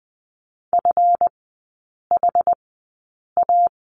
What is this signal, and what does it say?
Morse code, heard as a keyed tone, about 20 wpm, spaced as for 8 wpm, 700 Hz. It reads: FHA
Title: 138 Lesson 23e - Letter f - Three Characters 20wpm